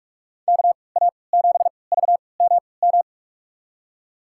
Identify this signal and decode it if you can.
Morse code, heard as a keyed tone, about 45 wpm, spaced as for 25 wpm, 700 Hz. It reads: KA7VMM
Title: USA Call Signs - 3x Character Spacing - 45wpm